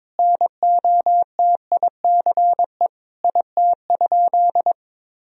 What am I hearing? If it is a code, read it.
Morse code, heard as a keyed tone, about 22 wpm, 700 Hz. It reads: NOTICE IT?
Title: Sentences from Top 1000 Words - Review Part 2 - 22wpm